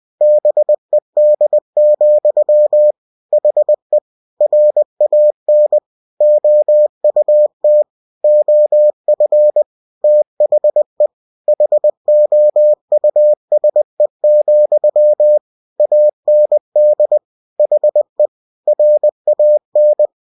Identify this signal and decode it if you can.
Morse code, heard as a keyed tone, about 20 wpm, 600 Hz. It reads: BED, HE RAN OUT OF THE HOUSE, AND HE RAN
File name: cw-books / winnie-the-pooh / winnie-20wpm-page-0040